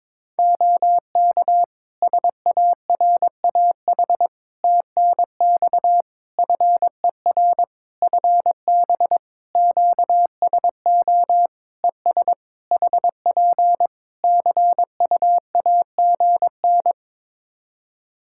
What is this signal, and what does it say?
Morse code, heard as a keyed tone, about 22 wpm, 700 Hz. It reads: OK SARAH TNX FER FB QSO ES HP CUAGN
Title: Practice QSO - Exchange 4 - Ending - 22wpm